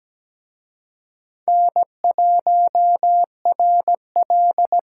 Morse code, heard as a keyed tone, about 20 wpm, 700 Hz. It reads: N1RL
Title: Speed Racing - USA 1x2 Call Signs 17wpm